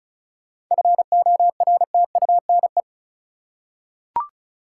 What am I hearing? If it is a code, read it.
Morse code, heard as a keyed tone, about 36 wpm, 700 Hz. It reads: FORTUNE
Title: Seven Letter Words - 35wpm